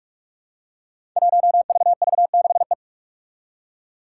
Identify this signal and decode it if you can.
Morse code, heard as a keyed tone, about 42 wpm, 700 Hz. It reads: 1VV6E
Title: Five Challenging Letter-Number Combinations 45wpm